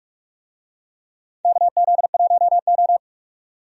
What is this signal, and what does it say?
Morse code, heard as a keyed tone, about 46 wpm, 700 Hz. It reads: KZ1O
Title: USA Call Signs - 4 Characters - 45wpm